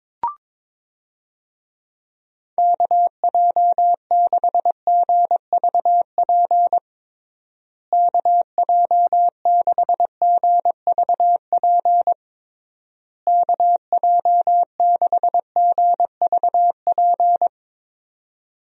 Morse code, heard as a keyed tone, about 21 wpm, 700 Hz. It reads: KJ6GVP KJ6GVP KJ6GVP
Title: USA Call Signs - 6 Characters - Repeated 3x - 22wpm